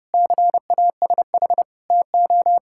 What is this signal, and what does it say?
Morse code, heard as a keyed tone, about 32 wpm, 700 Hz. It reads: CASH TO
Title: Sentences from Top 1000 Words - Review Part 6 - 30wpm